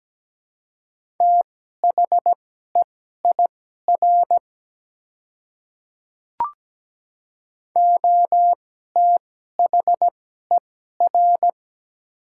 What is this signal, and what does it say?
Morse code, heard as a keyed tone, about 17 wpm, spaced as for 12 wpm, 690 Hz. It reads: THEIR OTHER
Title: 2x Character Spacing - Top 300 Words - 17wpm